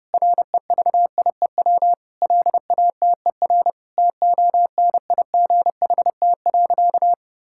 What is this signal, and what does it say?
Morse code, heard as a keyed tone, about 32 wpm, 700 Hz. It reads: REVIEW LATER TONIGHT.
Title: Sentences from Top 1000 Words - Review Part 6 - 30wpm